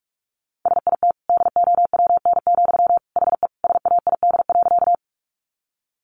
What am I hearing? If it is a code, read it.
Morse code, heard as a keyed tone, about 44 wpm, 700 Hz. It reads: SIT DOWN, HE SAID.
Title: Sentences from Top 500 Words 45wpm